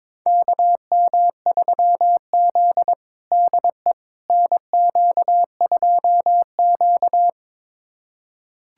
Morse code, heard as a keyed tone, about 22 wpm, 700 Hz. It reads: KM3Z DE NQ2Q
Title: USA Callsign DE Callsign - 4 Characters - 22wpm